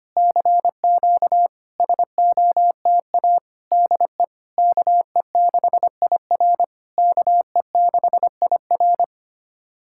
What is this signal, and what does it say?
Morse code, heard as a keyed tone, about 25 wpm, 700 Hz. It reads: CQ SOTA DE KE6IR KE6IR
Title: SOTA QSO Practice 25wpm